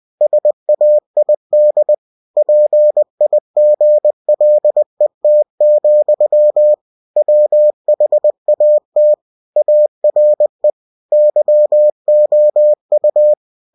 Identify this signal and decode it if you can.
Morse code, heard as a keyed tone, about 20 wpm, 600 Hz. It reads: SAID PIGLET, WHAT ARE YOU
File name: cw-books / winnie-the-pooh / winnie-20wpm-page-0021